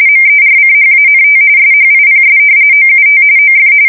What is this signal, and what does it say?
RTTY, about 45.45 baud. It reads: MERCHANT SHIPS OUT OF
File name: fsk170 / usnavy2